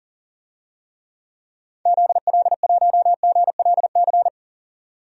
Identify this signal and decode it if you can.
Morse code, heard as a keyed tone, about 40 wpm, 700 Hz. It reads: ZL1GLC
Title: DXCC-100 Call Signs - 6 Characters - 40wpm